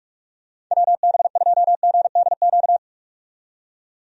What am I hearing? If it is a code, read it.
Morse code, heard as a keyed tone, about 46 wpm, 700 Hz. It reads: WB2GDQ
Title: USA Call Signs - 6 Characters - 45wpm